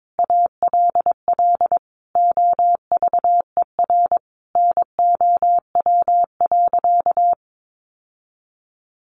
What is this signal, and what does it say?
Morse code, heard as a keyed tone, about 22 wpm, 700 Hz. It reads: ALL OVER NOW.
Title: Sentences from Top 1300 Words 22wpm